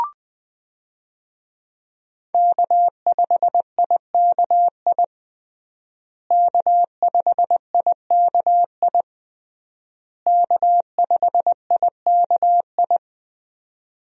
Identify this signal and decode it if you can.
Morse code, heard as a keyed tone, about 20 wpm, 700 Hz. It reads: K5IKI K5IKI K5IKI
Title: USA Call Signs - 5 Characters - Repeated 3x - 20wpm